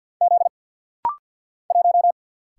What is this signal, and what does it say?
Morse code, heard as a keyed tone, about 48 wpm, 700 Hz. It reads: Z 1
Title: Single Letter-Number - Rapid-Fire 50wpm